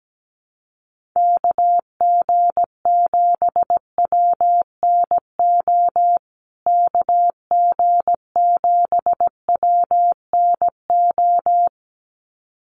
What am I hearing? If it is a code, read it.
Morse code, heard as a keyed tone, about 17 wpm, 700 Hz. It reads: KG7WNO KG7WNO
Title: USA Call Signs Repeated 2x 17wpm